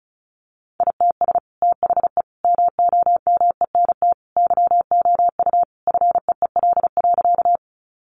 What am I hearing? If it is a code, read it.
Morse code, heard as a keyed tone, about 34 wpm, 700 Hz. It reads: ITS THE MOMENT YOU FEEL.
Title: Sentences from Top 1000 Words 35wpm